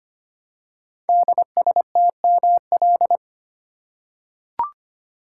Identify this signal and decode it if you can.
Morse code, heard as a keyed tone, about 25 wpm, 700 Hz. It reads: DSTML
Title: Code Groups - 5 Letters - 25wpm